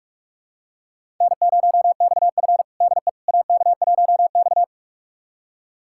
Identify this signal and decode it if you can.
Morse code, heard as a keyed tone, about 43 wpm, 700 Hz. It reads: N0XF DE AK1X
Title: USA Callsign DE Callsign - 4 Characters - 45wpm